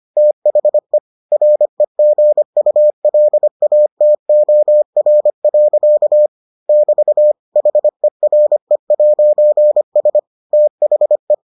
Morse code, heard as a keyed tone, about 25 wpm, 600 Hz. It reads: THE REGULATOR. <BT> HERE'S THE